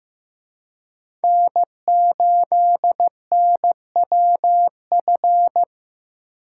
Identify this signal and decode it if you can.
Morse code, heard as a keyed tone, about 18 wpm, 700 Hz. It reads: N8NWF
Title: Speed Racing - USA 1x3 Call Signs 15wpm